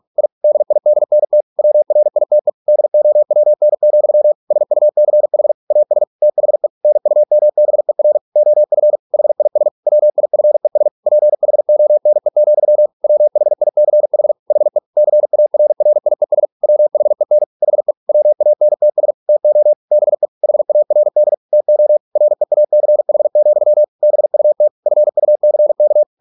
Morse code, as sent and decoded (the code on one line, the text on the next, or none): .. / -.. .. -.. -. - / .-- .-. .. - . / -.. --- .-- -. --..-- / ... ..- -.-. .... / .- ... / - .... / -. ..- -- -... .-. / --- ..-. / .... .. ... / .-- .. ..-. ... / .--. .... --- -. --..-- / .-- .... .. -.-. .... / .... / -.-. .- .-. .-. .. ... / .-- .... -. / .... / .-- .- -. - ... / - --- / -... / .... .- .-. -.. / - --- / .-. .- -.-. .... --..-- / -... ..- - / ..-. ..- -.-. -.-
I DIDNT WRITE DOWN, SUCH AS THE NUMBER OF HIS WIFES PHONE, WHICH HE CARRIES WHEN HE WANTS TO BE HARD TO REACH, BUT FUCK